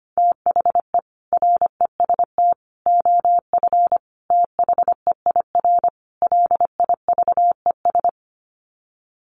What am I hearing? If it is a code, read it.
Morse code, heard as a keyed tone, about 25 wpm, 700 Hz. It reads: THE REST OF THEIR LIVES
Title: Sets of 5 Words Encore 25wpm